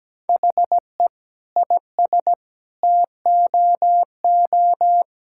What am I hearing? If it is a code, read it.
Morse code, heard as a keyed tone, about 17 wpm, 700 Hz. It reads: HE IS TOO
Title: Sentences from Top 1000 Words - Review Part 6 - 17wpm